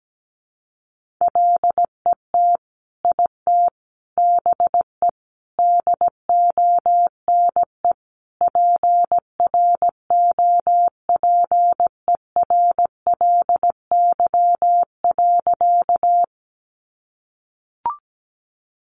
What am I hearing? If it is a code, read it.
Morse code, heard as a keyed tone, about 17 wpm, 700 Hz. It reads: LET IT BE DONE PROPERLY.
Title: Sentences from Top 1100 Words 17wpm